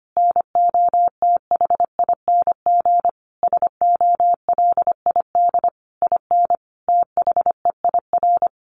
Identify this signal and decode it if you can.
Morse code, heard as a keyed tone, about 25 wpm, 700 Hz. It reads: NOTHING SOLID IN THEIR
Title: Sentences from Top 1500 Words 25wpm